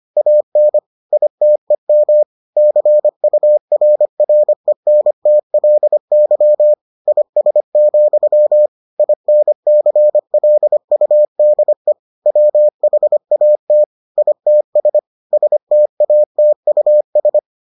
Morse code, heard as a keyed tone, about 25 wpm, 600 Hz. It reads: AN ITEM CURRENTLY IS, INCLUDE WHAT ITS STATUS